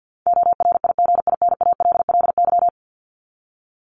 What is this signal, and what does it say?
Morse code, heard as a keyed tone, about 50 wpm, 700 Hz. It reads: ORIGINALLY
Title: Top 1251-1500 Words 50wpm